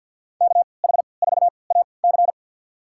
Morse code, heard as a keyed tone, about 53 wpm, spaced as for 29 wpm, 700 Hz. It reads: KH4AC
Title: USA Call Signs - 3x Character Spacing - 50wpm